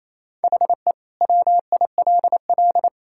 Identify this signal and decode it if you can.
Morse code, heard as a keyed tone, about 28 wpm, 700 Hz. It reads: HE WILL
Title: Sentences from Top 1000 Words - Review Part 5 - 28wpm